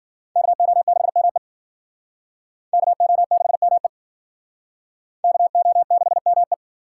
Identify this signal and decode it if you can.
Morse code, heard as a keyed tone, about 51 wpm, 700 Hz. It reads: KO6ME KO6ME KO6ME